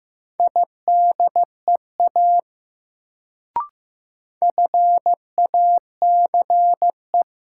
Morse code, heard as a keyed tone, about 15 wpm, 700 Hz. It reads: IDEA FACE
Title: Sets of 3 Words - New Word Drill - 15wpm